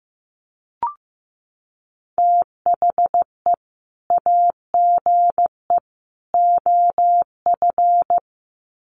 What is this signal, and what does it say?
Morse code, heard as a keyed tone, about 15 wpm, 700 Hz. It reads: THE AGE OF
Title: Sets of 3 Words 15wpm